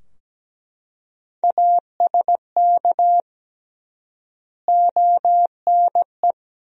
Morse code, heard as a keyed tone, about 17 wpm, 700 Hz. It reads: ASK ONE
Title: 3x Word-Spacing - Sentences from Top 300 Words - 17wpm